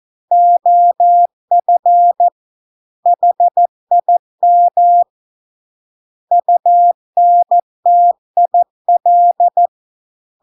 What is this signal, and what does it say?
Morse code, heard as a keyed tone, about 14 wpm, 700 Hz. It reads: OF HIM UNTIL